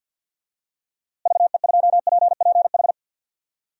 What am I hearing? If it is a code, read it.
Morse code, heard as a keyed tone, about 46 wpm, 700 Hz. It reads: VE2PPH
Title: DXCC-100 Call Signs - 6 Characters - Repeated 3x - 50wpm